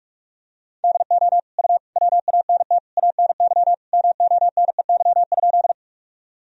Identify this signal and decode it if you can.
Morse code, heard as a keyed tone, about 41 wpm, 700 Hz. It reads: DO U WANT ANY MONEY?
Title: Sentences from Top 400 Words and Abbreviations 45wpm